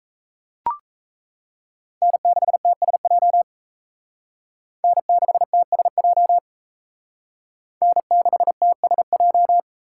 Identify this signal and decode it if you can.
Morse code, heard as a keyed tone, about 35 wpm, 700 Hz. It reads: N6TSJ N6TSJ N6TSJ
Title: Speed Racing - USA 1x3 Call Signs 28wpm